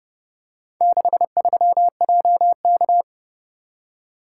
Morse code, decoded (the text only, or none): 63JK